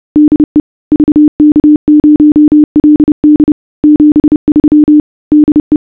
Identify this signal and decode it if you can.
Morse code, heard as a keyed tone, about 30 wpm, 300 Hz. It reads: DE VK0LD 73 DE